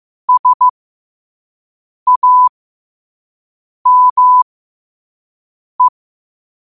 Morse code, heard as a keyed tone, about 15 wpm, spaced as for 5 wpm, 1000 Hz. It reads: SAME
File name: qotdfiles / 5wpm